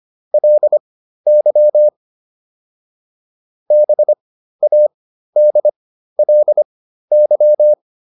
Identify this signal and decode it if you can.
Morse code, heard as a keyed tone, about 25 wpm, spaced as for 13 wpm, 600 Hz. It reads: LY BADLY
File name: LY5 13